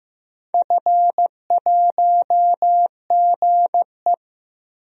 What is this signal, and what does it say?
Morse code, heard as a keyed tone, about 15 wpm, 700 Hz. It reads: F1GE
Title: DXCC-100 Call Signs - 4 Characters - Repeated 3x - 15wpm